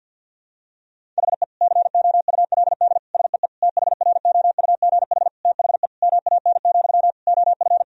Morse code, heard as a keyed tone, about 51 wpm, 700 Hz. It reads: HE COULD SEE THROUGH THE MAN, OF
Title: Sentences from Top 1000 Words - Review Part 1 - 50wpm